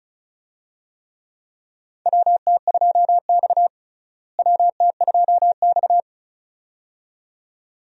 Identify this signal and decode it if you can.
Morse code, heard as a keyed tone, about 34 wpm, 700 Hz. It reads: WT2X WT2X
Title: USA Call Signs - 4 Characters - Repeated 2x - 35wpm